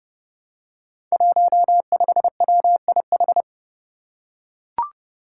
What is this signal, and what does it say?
Morse code, heard as a keyed tone, about 30 wpm, 700 Hz. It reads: 15WIH
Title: Five Challenging Letter-Number Combinations 30wpm